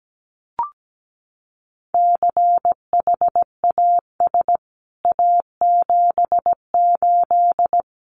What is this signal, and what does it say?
Morse code, heard as a keyed tone, about 17 wpm, 700 Hz. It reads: CHAS A78
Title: CWT Exchange 17wpm